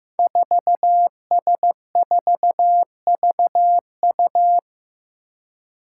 Morse code, heard as a keyed tone, about 15 wpm, 700 Hz. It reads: 4S4VU
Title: Five Challenging Letter-Number Combinations 15wpm